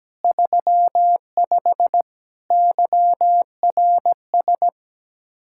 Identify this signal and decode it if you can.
Morse code, heard as a keyed tone, about 17 wpm, 700 Hz. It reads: 35 YRS
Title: K1USN SST QSO Practice 17wpm